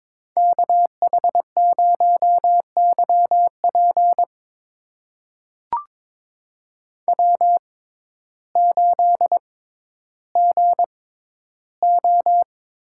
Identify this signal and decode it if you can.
Morse code, heard as a keyed tone, about 22 wpm, spaced as for 12 wpm, 700 Hz. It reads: KH0YP W8GO
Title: USA Call Signs - 5 Characters - 6x Character Spacing - 22wpm